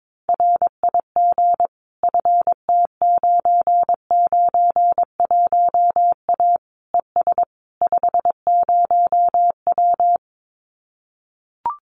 Morse code, heard as a keyed tone, about 22 wpm, 700 Hz. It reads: RIG FT991A ES 50W
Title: Common QSO Phrases Encore 22wpm